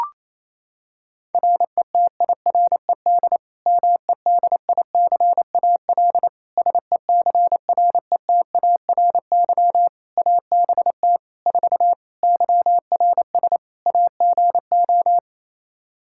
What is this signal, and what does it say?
Morse code, heard as a keyed tone, about 28 wpm, 700 Hz. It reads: RETIRED MEDICAL SECRETARY ABT 4 YRS AGO
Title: Practice QSO - Exchange 3 - More Info - 28wpm